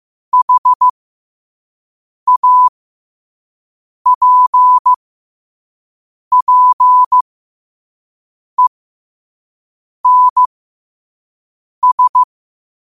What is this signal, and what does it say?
Morse code, heard as a keyed tone, about 15 wpm, spaced as for 5 wpm, 1000 Hz. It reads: HAPPENS
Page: New quotes every day in morse code at 5 Words per minute.